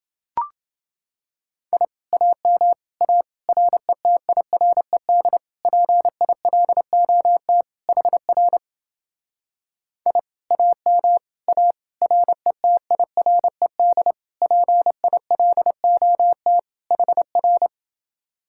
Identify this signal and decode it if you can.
Morse code, heard as a keyed tone, about 30 wpm, 700 Hz. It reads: I AM A RETIRED PILOT HR I AM A RETIRED PILOT HR